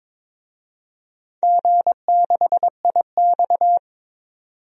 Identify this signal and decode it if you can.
Morse code, heard as a keyed tone, about 22 wpm, 700 Hz. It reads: G6IX
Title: DXCC-100 Call Signs - 4 Characters - 22wpm